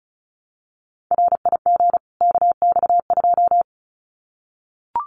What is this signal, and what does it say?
Morse code, heard as a keyed tone, about 34 wpm, 700 Hz. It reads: RIG KX2
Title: Common QSO Phrases 35wpm